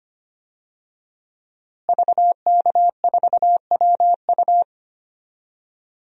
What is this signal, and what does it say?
Morse code, heard as a keyed tone, about 25 wpm, 700 Hz. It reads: VK4WU
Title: DXCC-100 Call Signs - 5 Characters - Repeated 3x - 25wpm